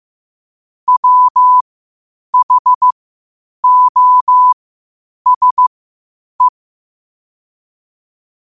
Morse code, heard as a keyed tone, about 15 wpm, spaced as for 8 wpm, 1000 Hz. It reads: WHOSE
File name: qotdfiles / 8wpm